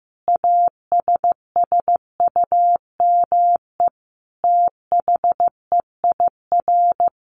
Morse code, heard as a keyed tone, about 15 wpm, 700 Hz. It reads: ASSUME THEIR
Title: Sentences from Top 1500 Words 15wpm